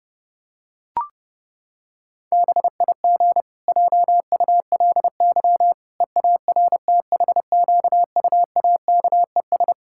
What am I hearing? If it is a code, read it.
Morse code, heard as a keyed tone, about 30 wpm, 700 Hz. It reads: BIG JULY EARTHQUAKES
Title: Pangram Challenge v2 - 30wpm